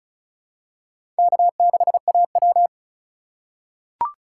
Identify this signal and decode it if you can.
Morse code, heard as a keyed tone, about 36 wpm, 700 Hz. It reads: K6AW
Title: USA Call Signs - 4 Characters - 2x Character Spacing - 35wpm